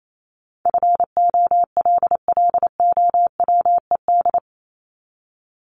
Morse code, heard as a keyed tone, about 28 wpm, 700 Hz. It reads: FOLLOWED
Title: ED Suffix - Top 100 Words with Base Words 28wpm